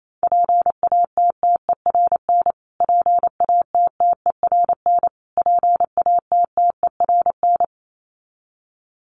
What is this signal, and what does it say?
Morse code, heard as a keyed tone, about 28 wpm, 700 Hz. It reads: PATTERN PATTERN PATTERN
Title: Top 500 Words Repeated 28wpm